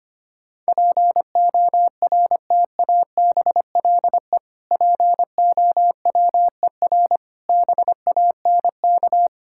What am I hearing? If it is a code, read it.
Morse code, heard as a keyed tone, about 25 wpm, 700 Hz. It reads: PORTABLE POWER BANK
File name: practice / 25wpm / Difficult Words with Hints - 25wpm